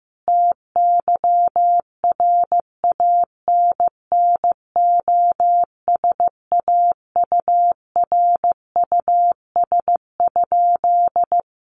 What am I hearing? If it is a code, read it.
Morse code, heard as a keyed tone, about 15 wpm, 700 Hz. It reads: TYRANNOSAURUS?
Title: A Sound Of Thunder 15wpm